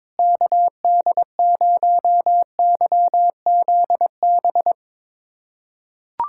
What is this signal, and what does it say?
Morse code, heard as a keyed tone, about 22 wpm, 700 Hz. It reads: KD0YZB
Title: USA Callsign DE Callsign - 6 Characters - 22wpm